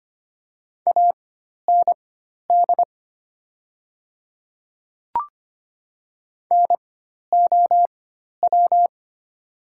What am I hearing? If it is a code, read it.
Morse code, heard as a keyed tone, about 25 wpm, spaced as for 12 wpm, 690 Hz. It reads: AND NOW
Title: 4x Character Spacing - Top 100 Words - 25wpm